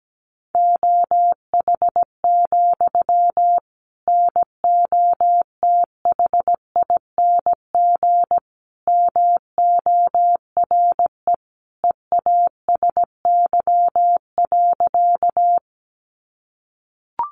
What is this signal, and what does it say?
Morse code, heard as a keyed tone, about 17 wpm, 700 Hz. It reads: OH, NOTHING MORE EASY.
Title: Sentences from Top 700 Words 17wpm